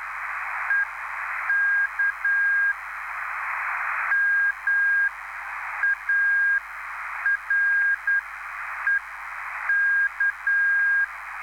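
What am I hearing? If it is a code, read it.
Morse code, heard as a keyed tone, about 10 wpm, spaced as for 7 wpm, 1600 Hz, with noise. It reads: E KMAREK
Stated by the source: Live CW QSO between PA2BX and SP3JFK on the 14055 kHz QRS frequency, at a basic 8 wpm.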